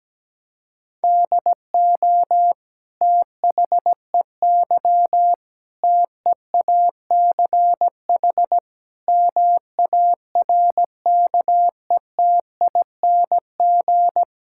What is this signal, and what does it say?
Morse code, heard as a keyed tone, about 17 wpm, 700 Hz. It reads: DO THEY TEACH MARKETING
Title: Sentences from Top 1000 Words - Review Part 5 - 17wpm